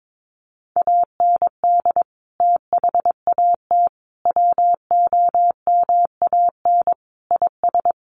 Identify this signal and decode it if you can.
Morse code, heard as a keyed tone, about 22 wpm, 700 Hz. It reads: AND THAT WOMAN IS
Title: Sentences from Top 1100 Words 22wpm